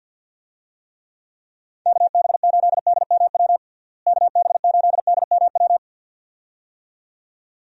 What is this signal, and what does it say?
Morse code, heard as a keyed tone, about 51 wpm, 700 Hz. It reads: KB8DMW KB8DMW
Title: USA Call Signs - 6 Characters - Repeated 2x - 50wpm